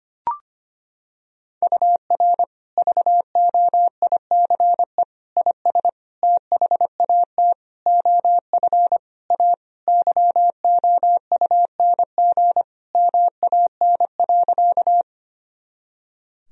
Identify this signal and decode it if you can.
Morse code, heard as a keyed tone, about 25 wpm, 700 Hz. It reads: UR VOICE IS THAT OF A YOUNG MAN.
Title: Sentences from Top 500 Words and Abbreviations 25wpm